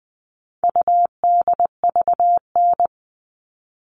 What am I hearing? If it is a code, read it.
Morse code, heard as a keyed tone, about 20 wpm, 700 Hz. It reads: UDVN
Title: All Cut Numbers - 4-Digit 20wpm